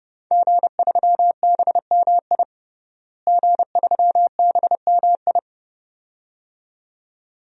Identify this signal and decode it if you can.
Morse code, heard as a keyed tone, about 30 wpm, 700 Hz. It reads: G3BMI G3BMI
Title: DXCC-100 Call Signs - 5 Characters - Repeated 2x - 30wpm